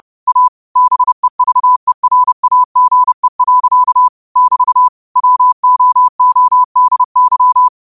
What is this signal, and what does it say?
Morse code, heard as a keyed tone, about 30 wpm, 1000 Hz. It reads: A BEVERAGE. <BT> WOODY